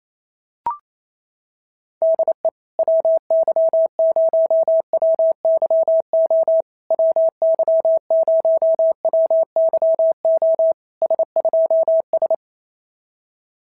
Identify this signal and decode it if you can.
Morse code, heard as a keyed tone, about 29 wpm, 650 Hz. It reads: DE WY0WYO WY0WYO S2S
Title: SOTA QSO Practice 28wpm